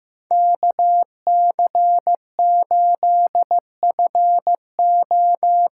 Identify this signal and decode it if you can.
Morse code, heard as a keyed tone, about 15 wpm, 700 Hz. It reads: KC8FO
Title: USA Call Signs - 6 Characters - 6x Character Spacing - 15wpm